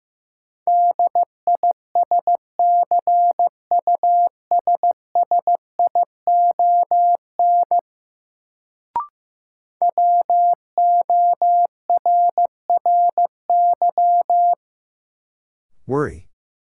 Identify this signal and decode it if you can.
Morse code, heard as a keyed tone, about 15 wpm, 700 Hz. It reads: DISCUSSION WORRY
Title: Sentences from Top 1100 Words - New Word Drill Alt - 15wpm